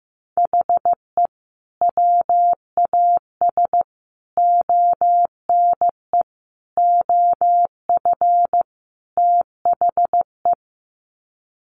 Morse code, heard as a keyed tone, about 15 wpm, 700 Hz. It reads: HE WAS ONE OF THE
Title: Sets of 5 Words 15wpm